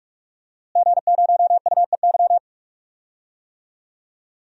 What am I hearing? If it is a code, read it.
Morse code, heard as a keyed tone, about 46 wpm, 700 Hz. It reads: G0UEY
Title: DXCC-100 Call Signs - 5 Characters - Repeated 2x - 45wpm